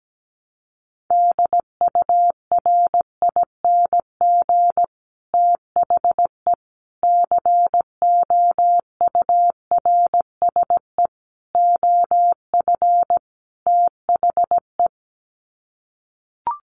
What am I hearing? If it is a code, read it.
Morse code, heard as a keyed tone, about 17 wpm, 690 Hz. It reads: DURING THE COURSE OF THE
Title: Sets of 5 Words Encore 17wpm